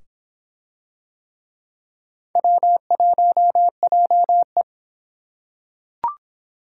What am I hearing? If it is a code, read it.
Morse code, heard as a keyed tone, about 26 wpm, 700 Hz. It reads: W1JE